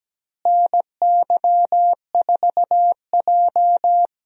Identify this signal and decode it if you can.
Morse code, heard as a keyed tone, about 17 wpm, 700 Hz. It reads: NY4J
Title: USA Call Signs - 4 Characters - Repeated 3x - 17wpm